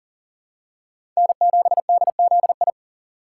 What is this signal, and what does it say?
Morse code, heard as a keyed tone, about 40 wpm, 700 Hz. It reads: N7DZI
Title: USA Call Signs - 5 Characters - 6x Character Spacing - 40wpm